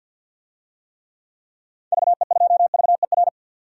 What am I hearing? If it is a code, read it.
Morse code, heard as a keyed tone, about 51 wpm, 700 Hz. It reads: VE2VER
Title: DXCC-100 Call Signs - 6 Characters - 50wpm